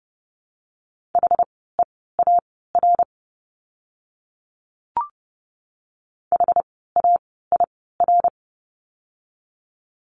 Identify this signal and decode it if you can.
Morse code, heard as a keyed tone, about 30 wpm, spaced as for 17 wpm, 700 Hz. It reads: HEAR HAIR
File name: Lesson 17f - Letter u - Words 30wpm